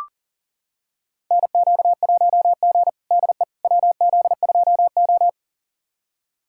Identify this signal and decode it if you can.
Morse code, heard as a keyed tone, about 37 wpm, 700 Hz. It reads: NQ1G DE WZ2O
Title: USA Callsign DE Callsign - 4 Characters - 40wpm